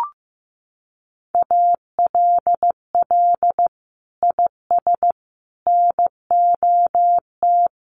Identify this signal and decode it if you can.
Morse code, heard as a keyed tone, about 15 wpm, 700 Hz. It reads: ALL IS NOT
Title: Sentences from Top 600 Words 15wpm